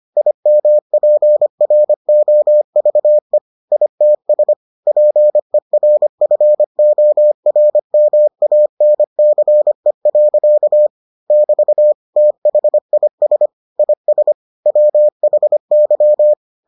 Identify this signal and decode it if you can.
Morse code, heard as a keyed tone, about 25 wpm, 600 Hz. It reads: IMPROVE ITS PERFORMANCE. <BT> THIS IS WHY